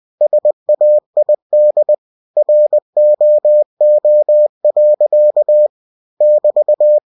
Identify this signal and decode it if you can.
Morse code, heard as a keyed tone, about 20 wpm, 600 Hz. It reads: SAID ROO. <BT>
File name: cw-books / winnie-the-pooh / winnie-20wpm-page-0106